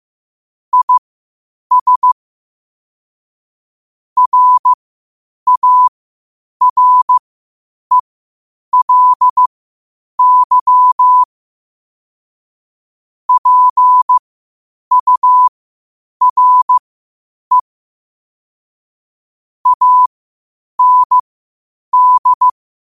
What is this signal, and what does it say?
Morse code, heard as a keyed tone, about 15 wpm, spaced as for 8 wpm, 1000 Hz. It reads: IS RARELY PURE AND
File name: qotdfiles / 8wpm